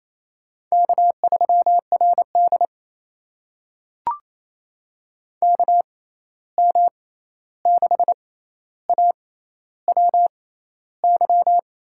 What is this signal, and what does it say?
Morse code, heard as a keyed tone, about 28 wpm, spaced as for 13 wpm, 700 Hz. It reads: K3RD KM6AWY
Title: USA Call Signs - 6x Character Spacing - 28wpm